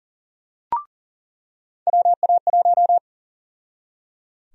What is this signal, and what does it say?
Morse code, heard as a keyed tone, about 37 wpm, 700 Hz. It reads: WA1
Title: USA Call Sign Prefixes - 40wpm